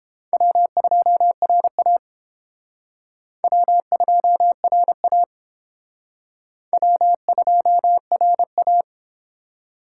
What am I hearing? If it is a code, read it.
Morse code, heard as a keyed tone, about 28 wpm, 700 Hz. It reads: W2RA W2RA W2RA